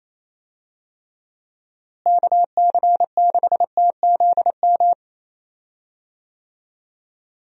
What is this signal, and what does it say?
Morse code, heard as a keyed tone, about 28 wpm, 700 Hz. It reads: KC6TZM
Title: USA Call Signs - 6 Characters - Repeated 2x - 28wpm